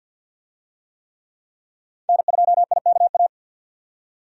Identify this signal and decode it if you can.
Morse code, heard as a keyed tone, about 47 wpm, 700 Hz. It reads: N2IKA
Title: USA Call Signs - 5 Characters - 50wpm